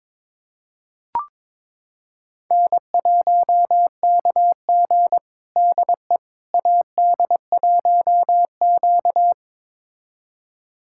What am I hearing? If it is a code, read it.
Morse code, heard as a keyed tone, about 22 wpm, 700 Hz. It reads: N1KG DE AD1Q
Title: USA Callsign DE Callsign - 4 Characters - 22wpm